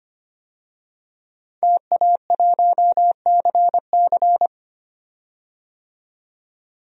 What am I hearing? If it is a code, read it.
Morse code, heard as a keyed tone, about 25 wpm, 700 Hz. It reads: TA1CC
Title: DXCC-100 Call Signs - 5 Characters - 25wpm